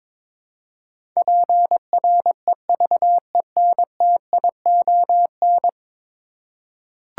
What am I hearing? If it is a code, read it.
Morse code, heard as a keyed tone, about 22 wpm, 700 Hz. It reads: PREVENTION
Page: morse-code-ninja-practice-22wpm